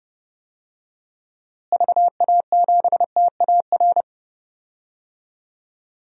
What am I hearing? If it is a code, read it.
Morse code, heard as a keyed tone, about 32 wpm, 700 Hz. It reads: VA7TAR
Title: DXCC-100 Call Signs - 6 Characters - Repeated 3x - 30wpm